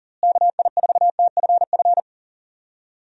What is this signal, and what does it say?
Morse code, heard as a keyed tone, about 40 wpm, 700 Hz. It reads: KI4TFF
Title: USA Call Signs - 6 Characters - Repeated 3x - 40wpm